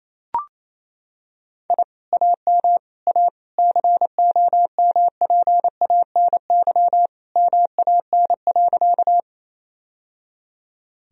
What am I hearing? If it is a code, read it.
Morse code, heard as a keyed tone, about 28 wpm, 700 Hz. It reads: I AM A COMPANY MAN.
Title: Sentences from Top 1000 Words - Review Part 3 - 28wpm